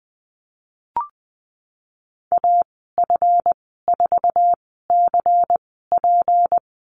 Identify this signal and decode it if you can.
Morse code, heard as a keyed tone, about 20 wpm, spaced as for 14 wpm, 700 Hz. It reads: AF4CP
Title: 173 Lesson 28g - Number 9 - Call Signs 20wpm